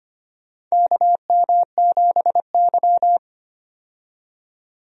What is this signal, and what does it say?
Morse code, heard as a keyed tone, about 25 wpm, 700 Hz. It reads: KM7Y
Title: USA Call Signs - 4 Characters - 25wpm